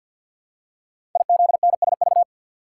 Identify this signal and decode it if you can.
Morse code, heard as a keyed tone, about 52 wpm, 700 Hz. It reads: IZNSV
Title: Code Groups - 5 Letters - 50wpm